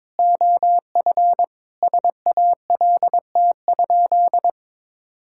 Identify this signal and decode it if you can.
Morse code, heard as a keyed tone, about 22 wpm, 700 Hz. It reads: OF SALT?
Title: Sentences from Top 1000 Words - Review Part 5 - 22wpm